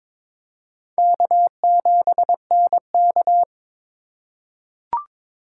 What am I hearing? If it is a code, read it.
Morse code, heard as a keyed tone, about 22 wpm, 700 Hz. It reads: K7NK
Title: USA Call Signs - 8x Character Spacing - 22wpm